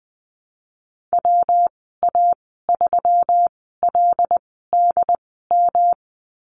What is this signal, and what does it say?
Morse code, heard as a keyed tone, about 20 wpm, spaced as for 15 wpm, 700 Hz. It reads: WA3LDM
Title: 147 Lesson 24g - Letter y - Call Signs 20wpm